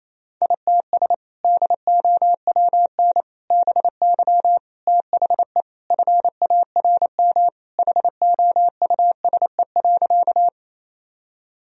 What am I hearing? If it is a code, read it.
Morse code, heard as a keyed tone, about 27 wpm, 700 Hz. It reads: ITS DOWN BY THE FARM HOUSE.
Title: Sentences from Top 1000 Words - Review Part 2 - 28wpm